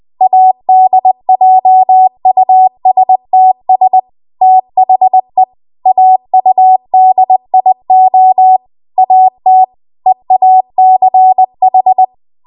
Morse code, heard as a keyed tone, about 20 wpm, 750 Hz. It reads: ADJUSTS THE AUDIO AT EACH